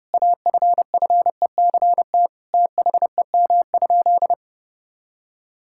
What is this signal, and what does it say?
Morse code, heard as a keyed tone, about 30 wpm, 700 Hz. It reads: AFFECT THEM?
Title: Sentences from Top 1000 Words - Review Part 6 - 30wpm